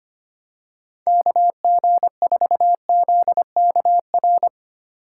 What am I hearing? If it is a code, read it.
Morse code, heard as a keyed tone, about 25 wpm, 700 Hz. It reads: KG4ZKR
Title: USA Call Signs - 6x Character Spacing - 25wpm